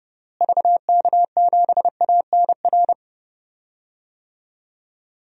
Morse code, heard as a keyed tone, about 30 wpm, 700 Hz. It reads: VK7ANR
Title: DXCC-100 Call Signs - 6 Characters - Repeated 2x - 30wpm